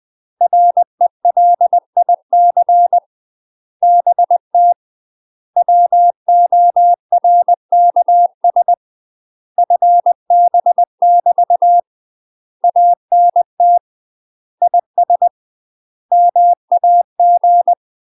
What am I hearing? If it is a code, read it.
Morse code, heard as a keyed tone, about 20 wpm, 700 Hz. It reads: RELIC BT WORKS FB <BT> ANT IS MAG
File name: UK MASTER QSOs TRAINING / 20WPM / UK MASTER QSOs 0059